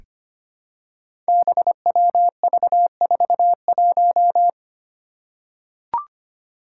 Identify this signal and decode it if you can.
Morse code, heard as a keyed tone, about 25 wpm, 700 Hz. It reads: BWV41
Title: Five Challenging Letter-Number Combinations 25wpm